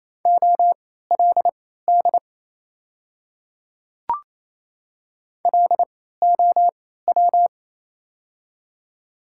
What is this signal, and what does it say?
Morse code, heard as a keyed tone, about 28 wpm, spaced as for 16 wpm, 700 Hz. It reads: OLD LOW
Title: 3x Character Spacing - Top 300 Words - 28wpm